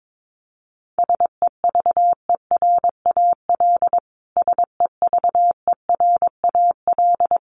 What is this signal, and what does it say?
Morse code, heard as a keyed tone, about 22 wpm, 700 Hz. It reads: SEVERAL SEVERAL
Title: Top 500 Words Repeated 22wpm